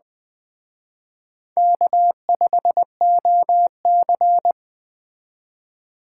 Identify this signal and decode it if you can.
Morse code, heard as a keyed tone, about 21 wpm, 700 Hz. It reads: K5OC
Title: Speed Racing - USA 1x2 Call Signs 17wpm